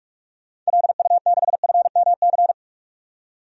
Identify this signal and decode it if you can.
Morse code, heard as a keyed tone, about 46 wpm, 700 Hz. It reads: LU6FMC
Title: DXCC-100 Call Signs - 6 Characters - 45wpm